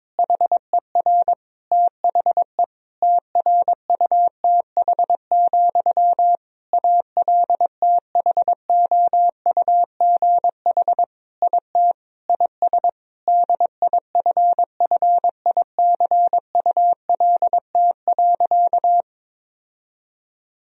Morse code, heard as a keyed tone, about 23 wpm, 700 Hz. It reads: HER THE TRUTH, ALTHOUGH IT IS DIFFICULT.
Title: Sentences from Top 1000 Words - Review Part 3 - 22wpm